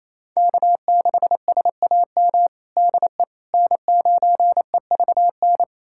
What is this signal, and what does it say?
Morse code, heard as a keyed tone, about 28 wpm, 700 Hz. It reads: K6SAM DE N9EVN
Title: USA Callsign DE Callsign - 5 Characters - 28wpm